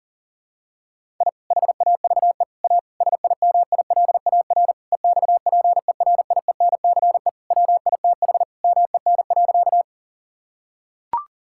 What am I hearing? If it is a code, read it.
Morse code, heard as a keyed tone, about 40 wpm, 700 Hz. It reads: I HAVE A SIMILAR EXPERIENCE WITH MEN.
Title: Sentences from Top 1000 Words - Review Part 5 - 40wpm